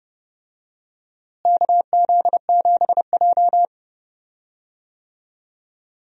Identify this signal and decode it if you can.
Morse code, heard as a keyed tone, about 32 wpm, 700 Hz. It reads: KZ7J
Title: USA Call Signs - 4 Characters - Repeated 3x - 30wpm